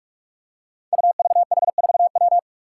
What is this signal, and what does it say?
Morse code, heard as a keyed tone, about 44 wpm, 700 Hz. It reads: UVH4W
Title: Five Challenging Letter-Number Combinations 45wpm